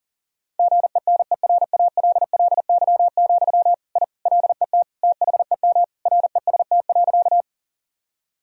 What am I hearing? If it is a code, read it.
Morse code, heard as a keyed tone, about 40 wpm, 700 Hz. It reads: GENERALLY, I LET THEM REST.
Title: Sentences from Top 1000 Words - Review Part 4 - 40wpm